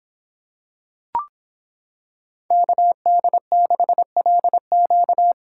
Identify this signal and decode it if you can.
Morse code, heard as a keyed tone, about 26 wpm, 700 Hz. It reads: KD6LQ